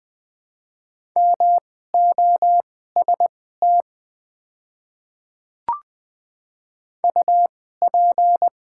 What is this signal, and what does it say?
Morse code, heard as a keyed tone, about 20 wpm, spaced as for 14 wpm, 700 Hz. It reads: MOST UP
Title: 2x Character Spacing - Top 100 Words - 20wpm